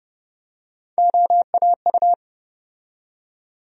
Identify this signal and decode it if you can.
Morse code, heard as a keyed tone, about 30 wpm, 700 Hz. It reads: OAU
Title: USA Call Sign Postfixes - 3 Letters 30wpm